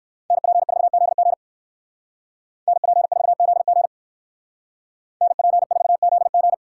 Morse code, heard as a keyed tone, about 62 wpm, 700 Hz. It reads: NP4ZG NP4ZG NP4ZG